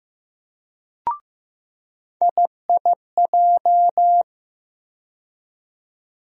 Morse code, heard as a keyed tone, about 15 wpm, 700 Hz. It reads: IIJ
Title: USA Call Sign Postfixes - 3 Letters 15wpm